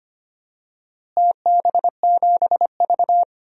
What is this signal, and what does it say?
Morse code, heard as a keyed tone, about 25 wpm, 700 Hz. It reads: TB7V
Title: DXCC-100 Call Signs - 4 Characters - Repeated 3x - 25wpm